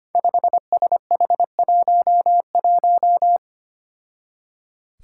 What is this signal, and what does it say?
Morse code, heard as a keyed tone, about 25 wpm, 700 Hz. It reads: 5SH11
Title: Five Challenging Letter-Number Combinations 25wpm